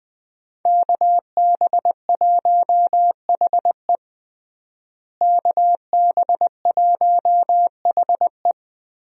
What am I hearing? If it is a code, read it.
Morse code, heard as a keyed tone, about 20 wpm, 700 Hz. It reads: KB1HE KB1HE
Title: USA Call Signs - 5 Characters - Repeated 3x - 20wpm